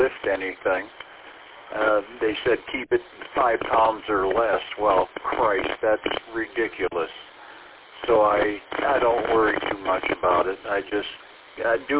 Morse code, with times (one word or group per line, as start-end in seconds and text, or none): none